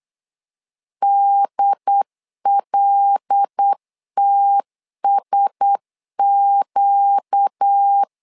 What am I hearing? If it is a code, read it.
Morse code, heard as a keyed tone, about 8 wpm, 800 Hz. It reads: DLTSQ
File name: LEZIONI / 5 LBVC / 5 L 8wpm 2831